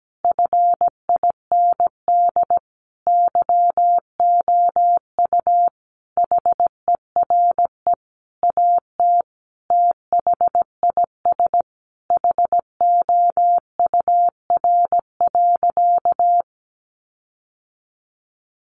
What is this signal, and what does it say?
Morse code, heard as a keyed tone, about 17 wpm, 700 Hz. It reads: FIND YOU HERE AT THIS HOUR.
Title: Sentences from Top 1400 Words 17wpm